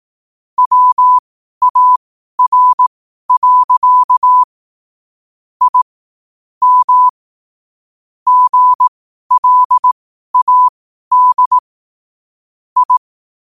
Morse code, heard as a keyed tone, about 18 wpm, spaced as for 12 wpm, 1000 Hz. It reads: WAR. I M GLAD I
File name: qotdfiles / 12wpm